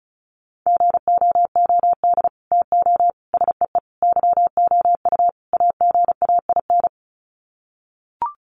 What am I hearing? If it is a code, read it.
Morse code, heard as a keyed tone, about 34 wpm, 700 Hz. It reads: GOOD TO SEE YOU AGAIN
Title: Sets of 5 Words Encore2 35wpm